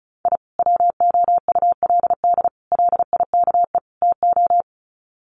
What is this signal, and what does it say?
Morse code, heard as a keyed tone, about 34 wpm, 700 Hz. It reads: I WOULD LIKE TO
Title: Sets of 4 Words 35wpm